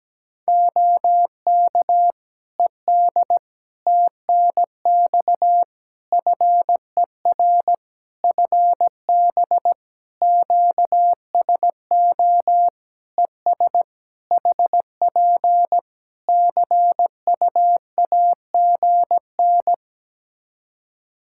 Morse code, heard as a keyed tone, about 17 wpm, 700 Hz. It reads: OK ED TNX FER FB QSO ES HP CUAGN
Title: Practice QSO - Exchange 4 - Ending - 17wpm